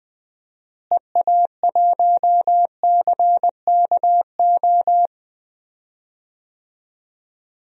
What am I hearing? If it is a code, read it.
Morse code, heard as a keyed tone, about 20 wpm, 700 Hz. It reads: EA1CKO
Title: DXCC-100 Call Signs - 6 Characters - Repeated 2x - 20wpm